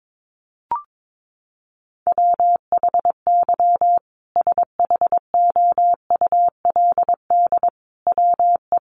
Morse code, heard as a keyed tone, about 22 wpm, 700 Hz. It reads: WHY SHOULD WE
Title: Sentences from Top 1500 Words 22wpm